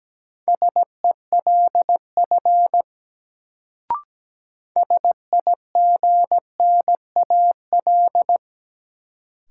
Morse code, heard as a keyed tone, about 17 wpm, 700 Hz. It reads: SELF SIGNAL
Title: Sentences from Top 1400 Words - New Word Drill Alt - 17wpm